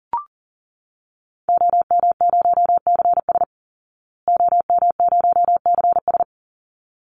Morse code, heard as a keyed tone, about 40 wpm, 700 Hz. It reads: OM0CS OM0CS
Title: Call Signs Repeated 3x - 40wpm